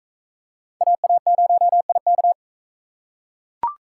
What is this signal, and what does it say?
Morse code, heard as a keyed tone, about 42 wpm, 700 Hz. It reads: AA0IK